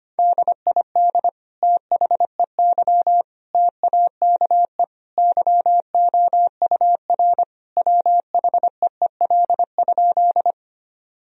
Morse code, heard as a keyed tone, about 25 wpm, 700 Hz. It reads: DID THEY TAKE YOUR WHEEL?
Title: Sentences from Top 1000 Words - Review Part 3 - 25wpm